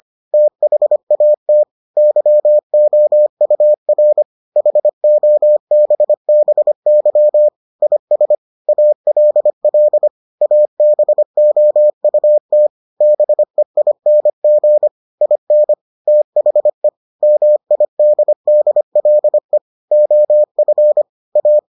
Morse code, heard as a keyed tone, about 25 wpm, 600 Hz. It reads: THAT YOUR HOBBY IS ALL ABOUT BEING IN THE MIDDLE OF A